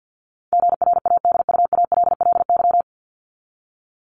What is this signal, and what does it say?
Morse code, heard as a keyed tone, about 50 wpm, 700 Hz. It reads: GRADUALLY
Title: LY Suffix - Top 100 Words with Base Words 50wpm